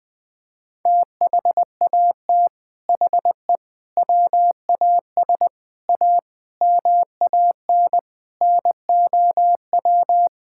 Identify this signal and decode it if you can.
Morse code, heard as a keyed tone, about 21 wpm, 700 Hz. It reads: THAT HE WAS A MAN NOW
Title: Sentences from Top 1000 Words - Review Part 6 - 20wpm